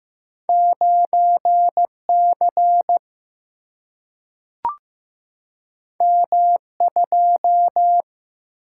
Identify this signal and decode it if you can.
Morse code, heard as a keyed tone, about 15 wpm, 700 Hz. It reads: 9 C M 2
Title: Single-Letter Single-Number 15wpm